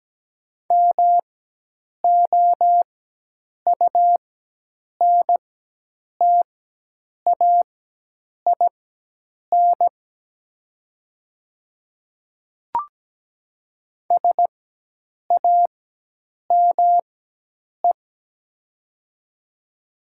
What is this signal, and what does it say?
Morse code, heard as a keyed tone, about 17 wpm, spaced as for 8 wpm, 700 Hz. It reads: MOUNTAIN SAME
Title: 4x Character Spacing - Top 500 Words - 17wpm